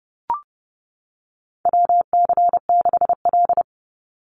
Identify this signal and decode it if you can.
Morse code, heard as a keyed tone, about 30 wpm, 700 Hz. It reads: WC6L
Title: USA Call Signs 30wpm